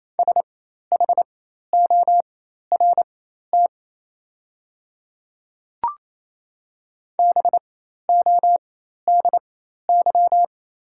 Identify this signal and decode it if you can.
Morse code, heard as a keyed tone, about 28 wpm, spaced as for 13 wpm, 700 Hz. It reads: SHORT BODY
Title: 4x Character Spacing - Top 500 Words - 28wpm